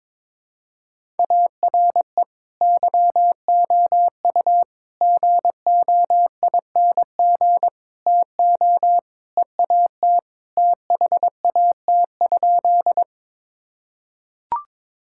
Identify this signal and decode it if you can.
Morse code, heard as a keyed tone, about 21 wpm, 700 Hz. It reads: ARE YOU GOING TO EAT THAT?
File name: practice / 22wpm / Sentences with 4-Word Phrases - 22wpm